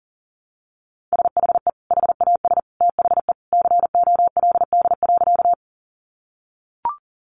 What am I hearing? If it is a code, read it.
Morse code, heard as a keyed tone, about 40 wpm, 700 Hz. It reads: SHE HAS THE COLD.
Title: Easy Phrases 40wpm